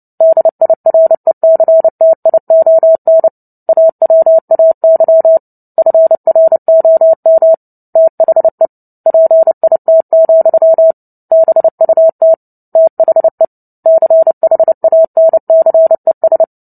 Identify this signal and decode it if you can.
Morse code, heard as a keyed tone, about 30 wpm, 650 Hz. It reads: DIRECTION AWAY FROM THE PIT, BUT THE CHANCES
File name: Book Two / 03-Chapter 3 - 29 WPM